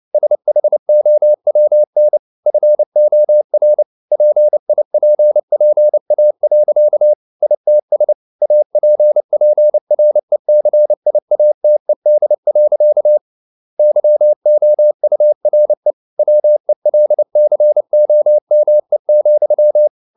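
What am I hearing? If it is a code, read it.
Morse code, heard as a keyed tone, about 29 wpm, 600 Hz. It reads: SHOWN FOR PIPPA. ITS APPRECIATED. YOURE WELCOME,
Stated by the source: Morse code practice at 28 wpm, text of Ken Follett's novel Never in English, from CW med Gnister.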